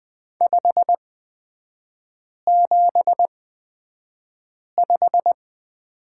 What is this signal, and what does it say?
Morse code, heard as a keyed tone, about 20 wpm, spaced as for 5 wpm, 700 Hz. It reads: 575
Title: 250 Lesson 40e - BK - Three Characters 20wpm